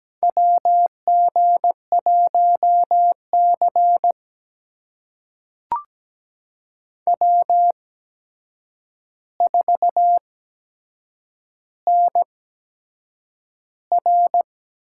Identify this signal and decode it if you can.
Morse code, heard as a keyed tone, about 17 wpm, spaced as for 7 wpm, 700 Hz. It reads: WG1C W4NR
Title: USA Call Signs - 4 Characters - 8x Character Spacing - 17wpm